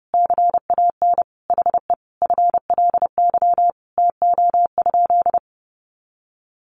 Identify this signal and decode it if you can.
Morse code, heard as a keyed tone, about 28 wpm, 700 Hz. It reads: CAN HE FLY TO?
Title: Sentences from Top 500 Words 30wpm